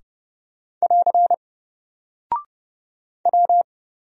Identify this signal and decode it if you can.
Morse code, heard as a keyed tone, about 29 wpm, 700 Hz. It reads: <AR> W
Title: Single Character (Letter:Number:Punctuation:Prosign) 30wpm